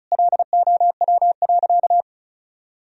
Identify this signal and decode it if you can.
Morse code, heard as a keyed tone, about 34 wpm, 700 Hz. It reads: LOW.
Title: Sentences from Top 1000 Words - Review Part 1 - 35wpm